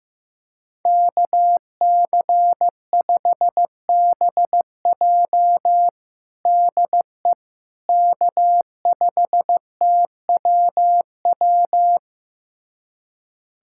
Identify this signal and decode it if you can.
Morse code, heard as a keyed tone, about 15 wpm, 700 Hz. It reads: KC5BJ DE K5TWW
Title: USA Callsign DE Callsign - 5 Characters - 15wpm